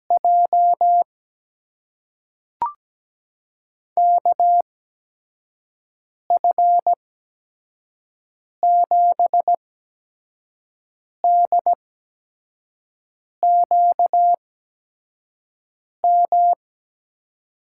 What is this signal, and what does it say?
Morse code, heard as a keyed tone, about 17 wpm, spaced as for 5 wpm, 700 Hz. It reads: J KF7DQM
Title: USA Call Signs - 6 Characters - 8x Character Spacing - 17wpm